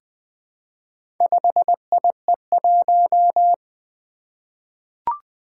Morse code, heard as a keyed tone, about 20 wpm, 700 Hz. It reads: 5IE1
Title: Four Letter-Number Combinations - 20wpm